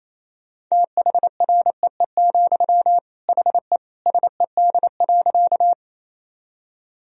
Morse code, heard as a keyed tone, about 28 wpm, 700 Hz. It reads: THREE, HE SED.
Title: Sentences from Top 200 Words and Abbreviations 28wpm